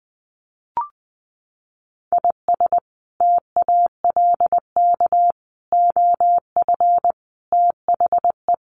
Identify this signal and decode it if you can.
Morse code, heard as a keyed tone, about 20 wpm, 700 Hz. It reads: IS TALK OF THE
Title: Sets of 5 Words Encore2 20wpm